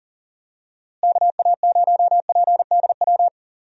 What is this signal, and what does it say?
Morse code, heard as a keyed tone, about 39 wpm, 700 Hz. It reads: KA0PDW
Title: USA Call Signs - 6 Characters - Repeated 3x - 40wpm